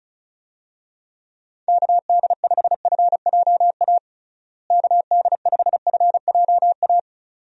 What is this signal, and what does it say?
Morse code, heard as a keyed tone, about 34 wpm, 700 Hz. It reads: KD5FJA KD5FJA
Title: USA Call Signs - 6 Characters - Repeated 2x - 35wpm